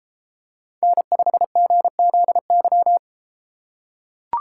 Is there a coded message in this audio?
Yes: Morse code from 0.8 to 3.0 s, heard as a keyed tone, about 31 wpm, 700 Hz. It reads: N5GZY